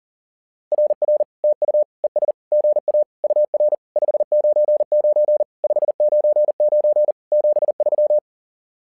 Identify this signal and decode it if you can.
Morse code, heard as a keyed tone, about 40 wpm, 600 Hz. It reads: RR TU ES GA UR 599 599 73
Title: SOTA QSO Practice 40wpm